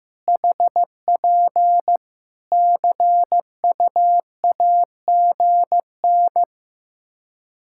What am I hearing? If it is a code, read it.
Morse code, heard as a keyed tone, about 15 wpm, 700 Hz. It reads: HP CUAGN
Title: Practice QSO - Exchange 4 - Ending - 15wpm